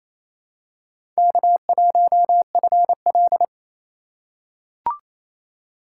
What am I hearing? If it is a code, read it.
Morse code, heard as a keyed tone, about 28 wpm, 700 Hz. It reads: K1FL
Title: USA Call Signs - 4x Character Spacing - 28wpm